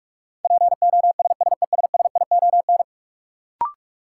Morse code, heard as a keyed tone, about 46 wpm, 700 Hz. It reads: POSSESSION
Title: Sentences from Top 1400 Words - New Word Drill - 45wpm